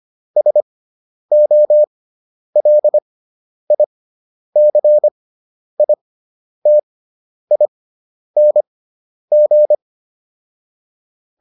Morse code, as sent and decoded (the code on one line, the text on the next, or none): ... --- .-.. .. -.-. .. - .. -. --.
SOLICITING